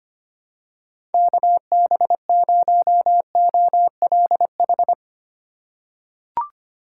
Morse code, heard as a keyed tone, about 25 wpm, 700 Hz. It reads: KB0OLH
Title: USA Call Signs - 6 Characters - 3x Character Spacing - 25wpm